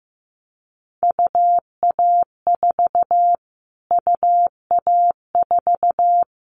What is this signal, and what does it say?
Morse code, heard as a keyed tone, about 15 wpm, 700 Hz. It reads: UA4 UA4
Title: Callsign Prefixes for Top 25 DXCC - Repeated 3x - 15wpm